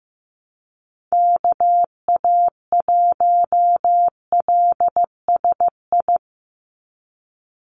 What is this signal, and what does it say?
Morse code, heard as a keyed tone, about 15 wpm, 700 Hz. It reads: KA1LSI
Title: USA Call Signs 15wpm